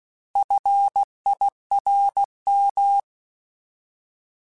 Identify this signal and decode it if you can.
Morse code, heard as a keyed tone, about 16 wpm, 800 Hz. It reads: FIRM
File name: Alphabetical / Alphabetical mp3s / Fs @15